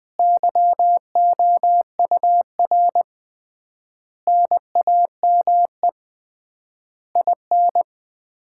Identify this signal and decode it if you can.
Morse code, heard as a keyed tone, about 20 wpm, 700 Hz. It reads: YOUR NAME IN
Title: Common QSO Phrases - 3x Word-Spacing - 20wpm